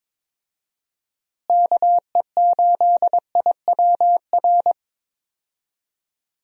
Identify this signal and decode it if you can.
Morse code, heard as a keyed tone, about 22 wpm, 700 Hz. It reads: KE8IWR
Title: USA Call Signs - 6 Characters - Repeated 2x - 22wpm